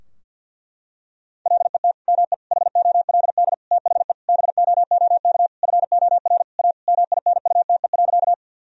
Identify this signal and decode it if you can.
Morse code, heard as a keyed tone, about 51 wpm, 700 Hz. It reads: LET ME HOLD THE BOOK FOR A MINUTE.
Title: Sentences from Top 1000 Words - Review Part 2 - 50wpm